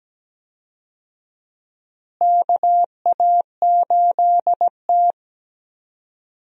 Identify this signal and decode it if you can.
Morse code, heard as a keyed tone, about 17 wpm, 700 Hz. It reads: KA8T
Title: USA Call Signs - 4 Characters - Repeated 3x - 17wpm